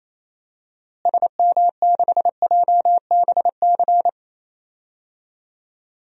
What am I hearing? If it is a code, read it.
Morse code, heard as a keyed tone, about 28 wpm, 700 Hz. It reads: SM6JBC
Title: DXCC-100 Call Signs - 6 Characters - Repeated 2x - 28wpm